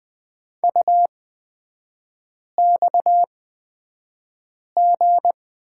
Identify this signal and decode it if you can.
Morse code, heard as a keyed tone, about 20 wpm, spaced as for 5 wpm, 690 Hz. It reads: UXG
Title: 229 Lesson 37e - Letter x - Three Characters 20wpm